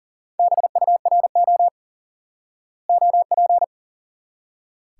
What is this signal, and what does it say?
Morse code, heard as a keyed tone, about 40 wpm, 700 Hz. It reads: BURO OP
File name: IQE Course 3 - Lesson 89 - YAGI - Two Elements - 40wpm